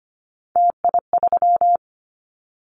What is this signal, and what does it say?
Morse code, heard as a keyed tone, about 25 wpm, 700 Hz. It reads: TI3
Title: Callsign Prefixes for Top 100 DXCC - 25wpm